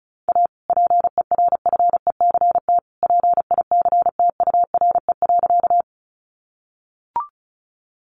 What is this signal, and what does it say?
Morse code, heard as a keyed tone, about 34 wpm, 700 Hz. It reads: A PERFECT PICTURE.
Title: Sentences from Top 1000 Words 35wpm